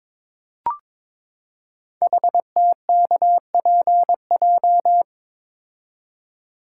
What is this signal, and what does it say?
Morse code, heard as a keyed tone, about 23 wpm, 700 Hz. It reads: HTKPJ
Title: Code Groups - 5 Letter:Numbers - 22wpm